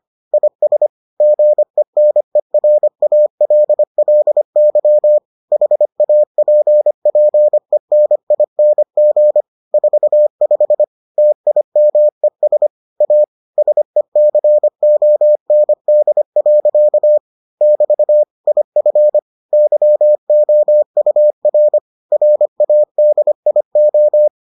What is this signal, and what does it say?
Morse code, heard as a keyed tone, about 25 wpm, 600 Hz. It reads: IS GENERALLY HAPPENING 45 TIMES A SECOND. <BT> IF YOUR RADIO